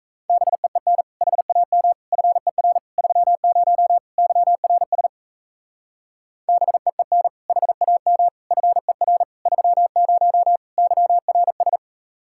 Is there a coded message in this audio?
Yes: Morse code from 0.3 to 11.8 s, heard as a keyed tone, about 35 wpm, 700 Hz. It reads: BEEN HAM FER 30 YRS BEEN HAM FER 30 YRS